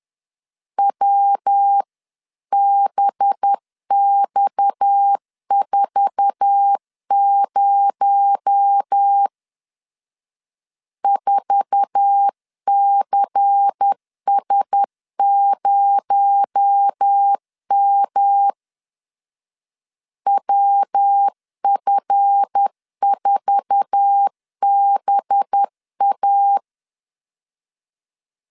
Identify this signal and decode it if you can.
Morse code, heard as a keyed tone, about 10 wpm, 800 Hz. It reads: WBX40 4CS0M WF4BA
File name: LEZIONI / 8 YJ45 / 8 YJ4 10wpm 2633